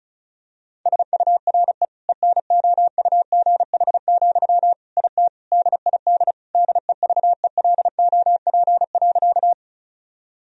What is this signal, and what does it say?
Morse code, heard as a keyed tone, about 36 wpm, 700 Hz. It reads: SURE ENOUGH, IT DID DEVELOP.
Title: Sentences from Top 1000 Words - Review Part 2 - 35wpm